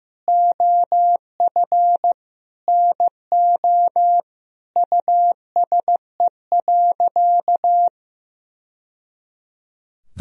Morse code, heard as a keyed tone, about 16 wpm, 700 Hz. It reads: OF NO USE.
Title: Sentences from Top 1000 Words - Review Part 4 - 15wpm